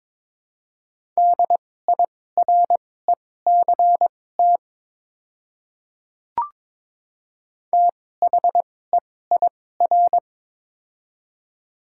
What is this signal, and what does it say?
Morse code, heard as a keyed tone, about 22 wpm, spaced as for 16 wpm, 700 Hz. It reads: DIRECT THEIR
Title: 2x Character Spacing - Top 500 Words - 22wpm